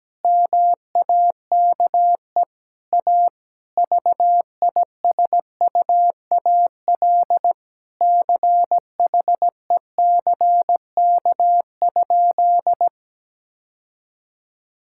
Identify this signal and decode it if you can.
Morse code, heard as a keyed tone, about 17 wpm, 700 Hz. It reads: MAKE A VISUAL CHECK?
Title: Sentences from Top 1000 Words - Review Part 6 - 17wpm